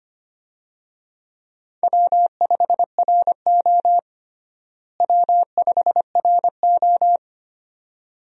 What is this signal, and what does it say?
Morse code, heard as a keyed tone, about 25 wpm, 700 Hz. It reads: W5RO W5RO
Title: USA Call Signs - 4 Characters - Repeated 2x - 25wpm